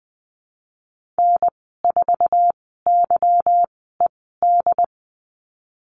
Morse code, heard as a keyed tone, about 20 wpm, spaced as for 15 wpm, 700 Hz. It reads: N4YED
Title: 245 Lesson 39g - Letter z - Call Signs 20wpm